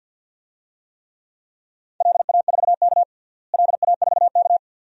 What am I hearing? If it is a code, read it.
Morse code, heard as a keyed tone, about 51 wpm, 700 Hz. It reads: LA4K LA4K
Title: DXCC-100 Call Signs - 4 Characters - Repeated 2x - 50wpm